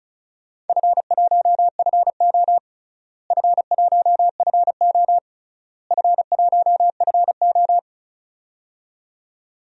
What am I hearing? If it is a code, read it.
Morse code, heard as a keyed tone, about 36 wpm, 690 Hz. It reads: F1FO F1FO F1FO
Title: DXCC-100 Call Signs - 4 Characters - Repeated 3x - 35wpm